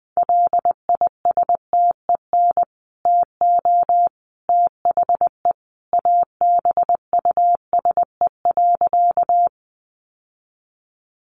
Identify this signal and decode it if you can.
Morse code, heard as a keyed tone, about 20 wpm, 700 Hz. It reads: LISTEN TO THE ABUSE.
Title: Sentences from Top 1300 Words 20wpm